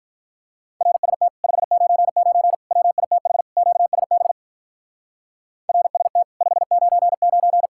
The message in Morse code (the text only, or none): RST 599 WITH QSB RST 599